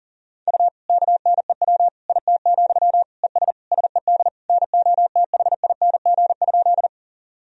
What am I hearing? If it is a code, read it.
Morse code, heard as a keyed tone, about 40 wpm, 700 Hz. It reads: U KNEW IT, ES SED NOTHING?
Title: Sentences from Top 500 Words and Abbreviations 40wpm